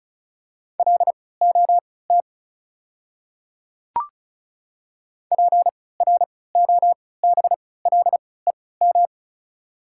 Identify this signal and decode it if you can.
Morse code, heard as a keyed tone, about 36 wpm, spaced as for 20 wpm, 700 Hz. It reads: LOT PROBLEM
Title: 3x Character Spacing - Top 500 Words - 35wpm